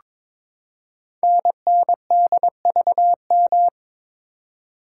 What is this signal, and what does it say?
Morse code, heard as a keyed tone, about 22 wpm, 700 Hz. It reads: NNDVM
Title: Code Groups - 5 Letters - 22wpm